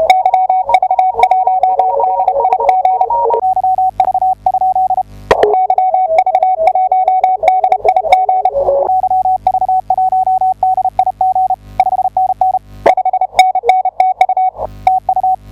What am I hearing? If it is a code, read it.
Morse code, heard as a keyed tone, about 32 wpm, 750 Hz. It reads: YV1DIG YV? YV1DIG YV1DIG 5NN 5NN TU TU
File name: tu2t / I1HJT 10m 28 ott